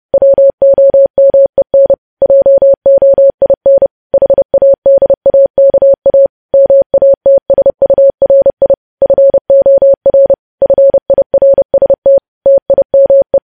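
Morse code, heard as a keyed tone, about 30 wpm, 550 Hz. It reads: WOMEN JOIN HADAKA MATSURI FOR FIRST TIME
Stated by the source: News headlines sent at 30 wpm.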